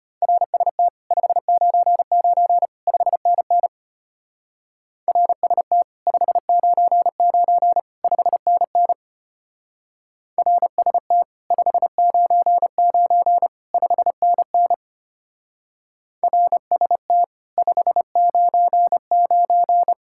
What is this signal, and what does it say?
Morse code, heard as a keyed tone, about 30 wpm, 700 Hz. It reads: RST 599 5NN RST 599 5NN RST 599 5NN RST 599
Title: Speed-Racing Common QSO Phrases - 25wpm